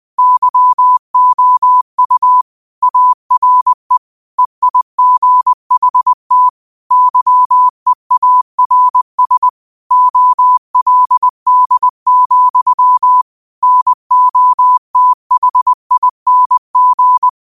Morse code, heard as a keyed tone, about 20 wpm, 1000 Hz. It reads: YOU ARE EIGHT YEARS OLD, NOTHING
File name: qotdfiles / 20wpm